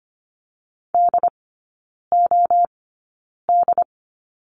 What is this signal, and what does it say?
Morse code, heard as a keyed tone, about 25 wpm, spaced as for 9 wpm, 700 Hz. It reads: DOD
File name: course / 25:10wpm / 093 Lesson 16e - Letter c - Three Characters 25wpm